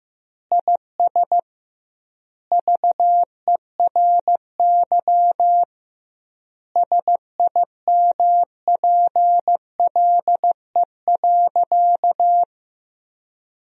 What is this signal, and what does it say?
Morse code, heard as a keyed tone, about 15 wpm, 700 Hz. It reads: IS VERY SIMPLE.
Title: 2x Word-Spacing - Sentences from Top 500 Words - 15wpm